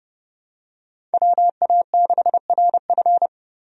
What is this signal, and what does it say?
Morse code, heard as a keyed tone, about 31 wpm, 700 Hz. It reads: WA6RF
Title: USA Call Signs - 5 Characters - 30wpm